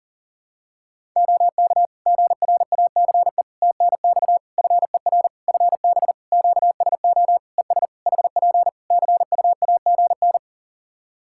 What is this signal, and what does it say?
Morse code, heard as a keyed tone, about 40 wpm, 700 Hz. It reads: OK GRACE TNX FER FB QSO ES HP CUAGN
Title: Practice QSO - Exchange 4 - Ending - 40wpm